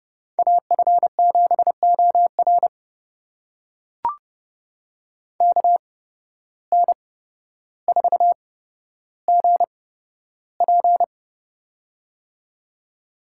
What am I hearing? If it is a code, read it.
Morse code, heard as a keyed tone, about 30 wpm, spaced as for 13 wpm, 700 Hz. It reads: AF7OR KN4GP
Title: USA Call Signs - 5 Characters - 8x Character Spacing - 30wpm